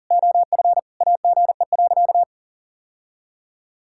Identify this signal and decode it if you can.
Morse code, heard as a keyed tone, about 40 wpm, 700 Hz. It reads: OF AGE.
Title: Sentences from Top 1000 Words - Review Part 6 - 40wpm